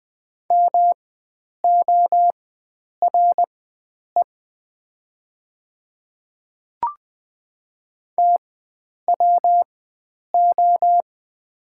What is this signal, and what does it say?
Morse code, heard as a keyed tone, about 20 wpm, spaced as for 9 wpm, 690 Hz. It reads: MORE TWO
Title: 4x Character Spacing - Top 100 Words - 20wpm